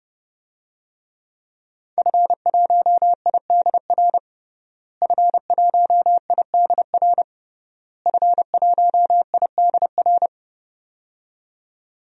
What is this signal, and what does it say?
Morse code, heard as a keyed tone, about 32 wpm, 700 Hz. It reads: F1IDR F1IDR F1IDR
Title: DXCC-100 Call Signs - 5 Characters - Repeated 3x - 30wpm